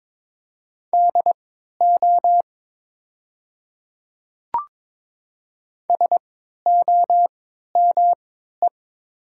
Morse code, heard as a keyed tone, about 22 wpm, spaced as for 12 wpm, 700 Hz. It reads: DO SOME
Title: 3x Character Spacing - Top 200 Words - 22wpm